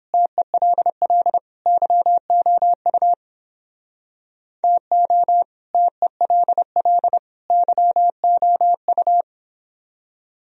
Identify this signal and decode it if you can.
Morse code, heard as a keyed tone, about 29 wpm, 700 Hz. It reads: TELL YOU TO TELL YOU